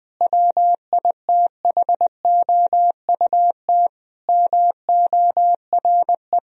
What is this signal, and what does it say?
Morse code, heard as a keyed tone, about 20 wpm, 700 Hz. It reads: WITHOUT MORE
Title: Sentences from Top 1000 Words - Review Part 3 - 20wpm